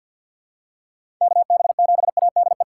Morse code, heard as a keyed tone, about 47 wpm, 700 Hz. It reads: KB7ADE
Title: USA Call Signs - 2x Character Spacing - 50wpm